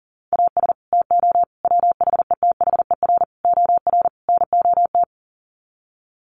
Morse code, heard as a keyed tone, about 40 wpm, 700 Hz. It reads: AS TO WHETHER OR NOT
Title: Sets of 5 Words Encore 40wpm